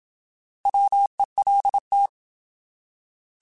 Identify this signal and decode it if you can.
Morse code, heard as a keyed tone, about 26 wpm, 800 Hz. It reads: WELT
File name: Alphabetical / Alphabetical mp3s / Ws @25